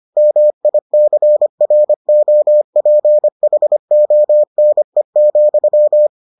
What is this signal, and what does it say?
Morse code, heard as a keyed tone, about 25 wpm, 600 Hz. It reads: MICROPHONE,